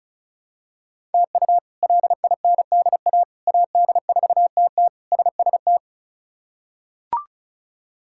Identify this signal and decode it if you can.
Morse code, heard as a keyed tone, about 36 wpm, 700 Hz. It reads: TU LINDA AD4TT SST
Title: K1USN SST QSO Practice - 35wpm